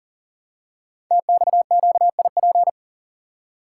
Morse code, heard as a keyed tone, about 42 wpm, 690 Hz. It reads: TXQIP
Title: Code Groups - 5 Letters - 40wpm